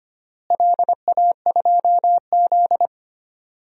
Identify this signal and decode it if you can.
Morse code, heard as a keyed tone, about 25 wpm, 700 Hz. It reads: LA2Z
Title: DXCC-100 Call Signs - 4 Characters - Repeated 3x - 25wpm